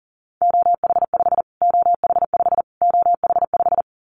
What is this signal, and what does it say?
Morse code, heard as a keyed tone, about 40 wpm, 700 Hz. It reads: OH5 OH5 OH5
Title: Callsign Prefixes for Top 25 DXCC - Repeated 3x - 40wpm